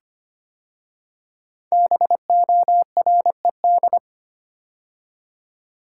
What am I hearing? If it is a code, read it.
Morse code, heard as a keyed tone, about 25 wpm, 700 Hz. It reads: BORED
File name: 5 Letter Words - First 3 Letters Given - 25wpm